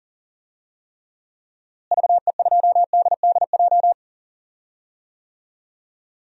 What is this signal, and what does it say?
Morse code, heard as a keyed tone, about 40 wpm, 700 Hz. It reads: VE2DDJ
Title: DXCC-100 Call Signs - 6 Characters - Repeated 3x - 40wpm